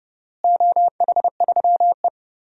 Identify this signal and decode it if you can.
Morse code, heard as a keyed tone, about 32 wpm, 700 Hz. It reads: OH3E
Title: DXCC-100 Call Signs - 4 Characters - Repeated 3x - 30wpm